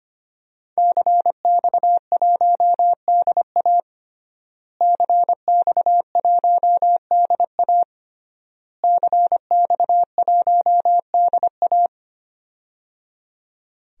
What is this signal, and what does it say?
Morse code, heard as a keyed tone, about 25 wpm, 700 Hz. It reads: CX1DA CX1DA CX1DA
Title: DXCC-100 Call Signs - 5 Characters - Repeated 3x - 25wpm